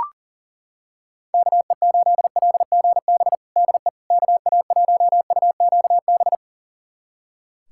Morse code, heard as a keyed tone, about 40 wpm, 700 Hz. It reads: KE8LGB DE KA1UQB
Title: USA Callsign DE Callsign - 6 Characters - 40wpm